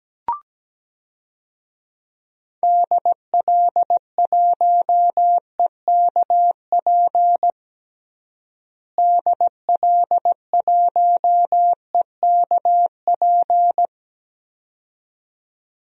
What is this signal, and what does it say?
Morse code, heard as a keyed tone, about 17 wpm, 700 Hz. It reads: DL1EKP DL1EKP
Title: DXCC-100 Call Signs - 6 Characters - Repeated 2x - 17wpm